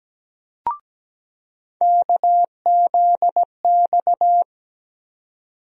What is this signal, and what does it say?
Morse code, heard as a keyed tone, about 17 wpm, 700 Hz. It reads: KZX
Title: USA Call Sign Postfixes - 3 Letters 17wpm